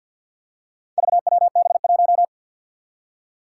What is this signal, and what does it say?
Morse code, heard as a keyed tone, about 51 wpm, 700 Hz. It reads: VWB1
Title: Four Challenging Letter-Number Combinations 50wpm